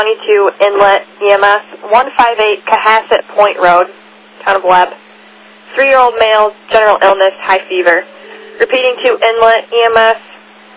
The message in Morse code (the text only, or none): none